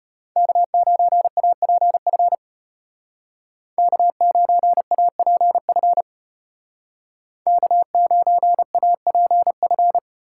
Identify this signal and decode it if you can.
Morse code, heard as a keyed tone, about 35 wpm, 700 Hz. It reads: K9APF K9APF K9APF